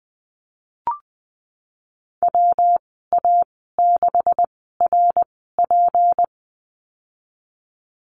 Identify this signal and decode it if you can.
Morse code, heard as a keyed tone, about 20 wpm, spaced as for 15 wpm, 700 Hz. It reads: WA6RP
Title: 167 Lesson 27g - Number 7 - Call Signs 20wpm